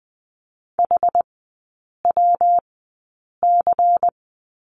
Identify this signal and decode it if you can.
Morse code, heard as a keyed tone, about 20 wpm, spaced as for 8 wpm, 700 Hz. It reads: HWC
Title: 133 Lesson 22e - Question Mark - Three Characters 20wpm